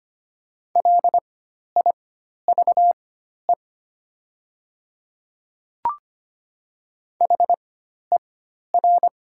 Morse code, heard as a keyed tone, about 25 wpm, spaced as for 12 wpm, 700 Hz. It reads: LIVE HER
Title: 4x Character Spacing - Top 200 Words - 25wpm